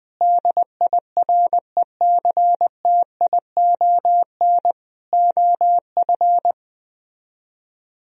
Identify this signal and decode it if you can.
Morse code, heard as a keyed tone, about 20 wpm, 700 Hz. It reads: DIRECTION OF